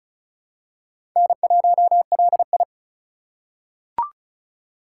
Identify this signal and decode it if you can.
Morse code, heard as a keyed tone, about 36 wpm, 700 Hz. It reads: N1LI
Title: USA Call Signs - 4 Characters - 4x Character Spacing - 35wpm